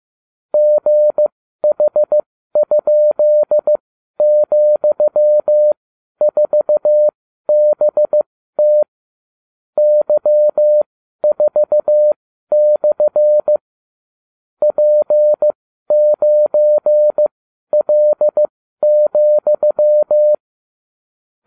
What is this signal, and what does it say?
Morse code, heard as a keyed tone, about 15 wpm, spaced as for 12 wpm, 600 Hz. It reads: GH?,4BT Y4/ P9L,